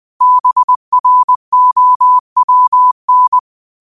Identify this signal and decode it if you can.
Morse code, heard as a keyed tone, about 20 wpm, 1000 Hz. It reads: BROWN
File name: qotdfiles / 20wpm